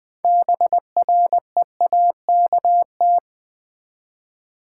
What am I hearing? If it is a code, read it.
Morse code, heard as a keyed tone, about 20 wpm, 700 Hz. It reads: BREAKT
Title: Build A Word - 6 Letters At A Time - 12-Letter Words - 20wpm